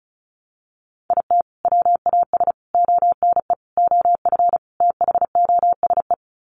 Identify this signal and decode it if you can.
Morse code, heard as a keyed tone, about 34 wpm, 700 Hz. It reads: IT WAS ONE OF THOSE
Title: Sets of 5 Words Encore 35wpm